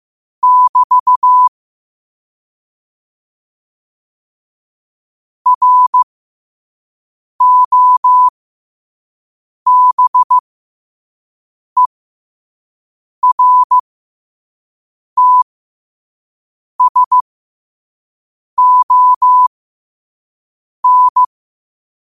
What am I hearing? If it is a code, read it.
Morse code, heard as a keyed tone, about 15 wpm, spaced as for 5 wpm, 1000 Hz. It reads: <BT> ROBERTSON
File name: qotdfiles / 5wpm